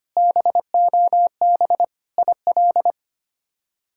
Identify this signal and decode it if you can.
Morse code, heard as a keyed tone, about 25 wpm, 700 Hz. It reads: BOB IL
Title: K1USN SST QSO Practice - 25wpm